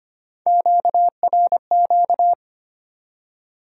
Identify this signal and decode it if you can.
Morse code, heard as a keyed tone, about 25 wpm, 700 Hz. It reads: QRQ
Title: Top 90 QSO Elements 25wpm